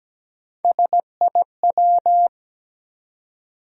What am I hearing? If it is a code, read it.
Morse code, heard as a keyed tone, about 17 wpm, 700 Hz. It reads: SIW
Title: USA Call Sign Postfixes - 3 Letters 17wpm